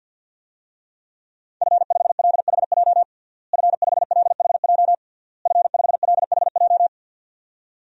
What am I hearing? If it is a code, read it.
Morse code, heard as a keyed tone, about 48 wpm, 700 Hz. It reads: F5LHJ F5LHJ F5LHJ
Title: DXCC-100 Call Signs - 5 Characters - Repeated 3x - 50wpm